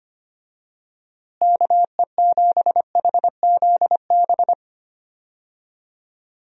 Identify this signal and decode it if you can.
Morse code, heard as a keyed tone, about 25 wpm, 700 Hz. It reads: KE7HZB
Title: USA Call Signs - 6 Characters - Repeated 2x - 25wpm